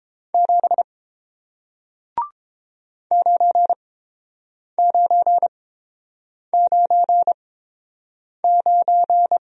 Morse code, heard as a keyed tone, about 30 wpm, spaced as for 7 wpm, 700 Hz. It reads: 7 9999